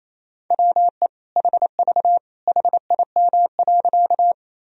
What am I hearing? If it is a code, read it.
Morse code, heard as a keyed tone, about 28 wpm, 700 Hz. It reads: WE HV HIM.
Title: Sentences from Top 100 Words and Abbreviations 28wpm